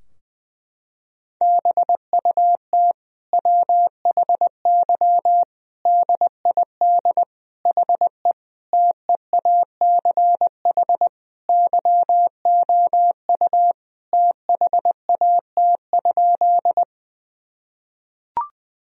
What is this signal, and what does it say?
Morse code, heard as a keyed tone, about 20 wpm, 700 Hz. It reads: BUT WHY DID HE TEACH YOU THAT?
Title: Sentences from Top 1000 Words - Review Part 2 - 20wpm